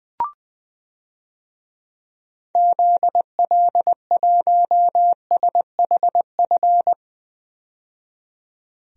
Morse code, heard as a keyed tone, about 20 wpm, 700 Hz. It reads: ZL1SHF
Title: DXCC-100 Call Signs - 6 Characters - 20wpm